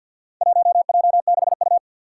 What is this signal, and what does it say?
Morse code, heard as a keyed tone, about 47 wpm, 700 Hz. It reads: 1J6U
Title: Four Challenging Letter-Number Combinations 50wpm